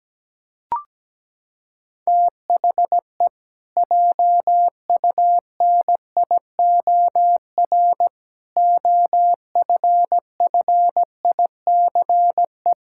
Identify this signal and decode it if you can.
Morse code, heard as a keyed tone, about 17 wpm, 700 Hz. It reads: THE JUNIOR OFFICE
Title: Pangram Challenge v2 - 17wpm